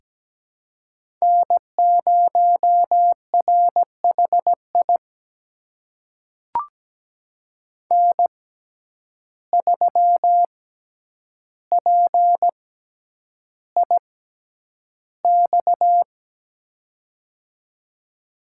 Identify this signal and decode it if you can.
Morse code, heard as a keyed tone, about 17 wpm, spaced as for 9 wpm, 700 Hz. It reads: N0RHI N3PIX
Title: USA Call Signs - 5 Characters - 6x Character Spacing - 17wpm